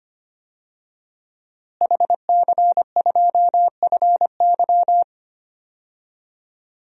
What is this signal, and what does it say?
Morse code, heard as a keyed tone, about 25 wpm, 700 Hz. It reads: HC2FY
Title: DXCC-100 Call Signs - 5 Characters - Repeated 2x - 25wpm